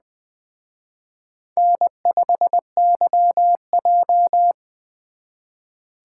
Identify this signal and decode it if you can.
Morse code, heard as a keyed tone, about 23 wpm, 700 Hz. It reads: N5YJ
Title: Speed Racing - USA 1x2 Call Signs 20wpm